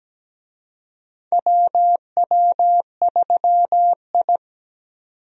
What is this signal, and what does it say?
Morse code, heard as a keyed tone, about 17 wpm, 700 Hz. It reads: WW3I
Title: USA Call Signs - 4 Characters - 17wpm